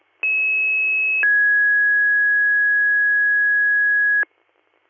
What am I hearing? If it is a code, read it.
Two-tone page, heard as a one-second tone, then a three-second tone, about 2600 then 1650 Hz. It is heard from 0.2 to 4.2 s.